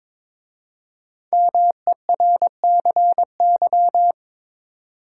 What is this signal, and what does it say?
Morse code, heard as a keyed tone, about 23 wpm, 700 Hz. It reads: MERCY
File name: 5 Letter Words - First 3 Letters Given - 22wpm